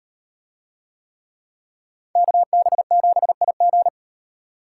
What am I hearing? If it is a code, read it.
Morse code, heard as a keyed tone, about 38 wpm, 700 Hz. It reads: KB7IG